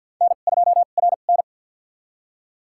Morse code, heard as a keyed tone, about 52 wpm, spaced as for 37 wpm, 700 Hz. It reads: N2RN
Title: USA Call Signs - 4 Characters - 2x Character Spacing - 50wpm